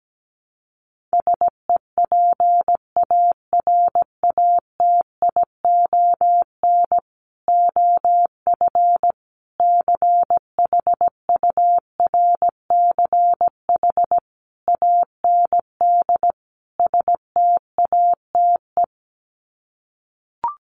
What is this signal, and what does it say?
Morse code, heard as a keyed tone, about 17 wpm, 700 Hz. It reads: SEPARATION OF CHURCH AND STATE
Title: Sets of 5 Words 17wpm